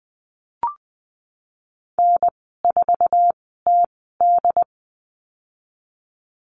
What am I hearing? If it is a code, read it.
Morse code, heard as a keyed tone, about 20 wpm, spaced as for 14 wpm, 700 Hz. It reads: N4TD
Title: 179 Lesson 29g - Slash - Call Signs 20wpm